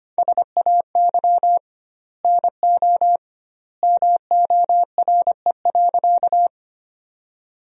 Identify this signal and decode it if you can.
Morse code, heard as a keyed tone, about 25 wpm, 700 Hz. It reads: SAY NO MORE.
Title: 2x Word-Spacing - Sentences from Top 200 Words - 25wpm